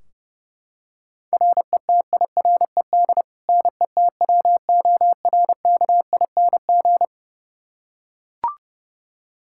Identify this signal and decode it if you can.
Morse code, heard as a keyed tone, about 30 wpm, 700 Hz. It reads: RETIRED NETWORKING
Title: K1USN SST QSO Practice 30wpm